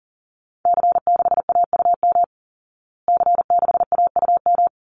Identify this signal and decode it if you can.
Morse code, heard as a keyed tone, about 40 wpm, 700 Hz. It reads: C6AUM C6AUM
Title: Call Signs Repeated 3x - 40wpm